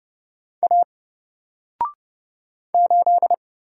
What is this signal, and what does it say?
Morse code, heard as a keyed tone, about 30 wpm, 700 Hz. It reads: A 8
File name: Single-Letter-Number-30wpm